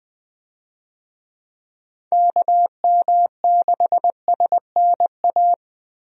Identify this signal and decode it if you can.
Morse code, heard as a keyed tone, about 20 wpm, 700 Hz. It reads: KM6SNA
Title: USA Call Signs - 6 Characters - Repeated 3x - 20wpm